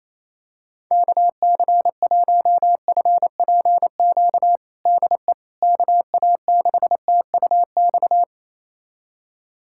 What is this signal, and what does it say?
Morse code, heard as a keyed tone, about 27 wpm, 700 Hz. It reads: KC1FPQ DE KA6TUX
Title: USA Callsign DE Callsign 28wpm